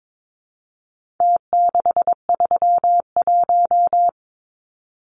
Morse code, heard as a keyed tone, about 22 wpm, 700 Hz. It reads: T631
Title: Numbers - Contest Style - 4-Digit 22wpm